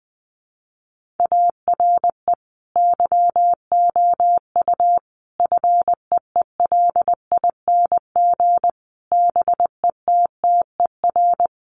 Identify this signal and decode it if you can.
Morse code, heard as a keyed tone, about 20 wpm, 700 Hz. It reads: ARE YOU FEELING BETTER
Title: Sentences from Top 1200 Words 20wpm